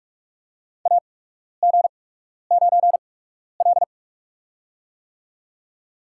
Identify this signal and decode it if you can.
Morse code, heard as a keyed tone, about 43 wpm, spaced as for 12 wpm, 690 Hz. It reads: AG9L
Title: USA Call Signs - 4 Characters - 8x Character Spacing - 45wpm